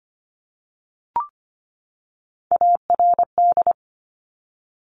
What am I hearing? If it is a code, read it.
Morse code, heard as a keyed tone, about 25 wpm, 700 Hz. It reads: ARD
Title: Top Three Letter Combinations in Words 25wpm